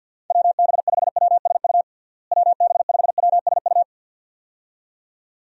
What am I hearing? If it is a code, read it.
Morse code, heard as a keyed tone, about 50 wpm, 700 Hz. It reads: WB5WSU WB5WSU
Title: USA Call Signs - 6 Characters - Repeated 2x - 50wpm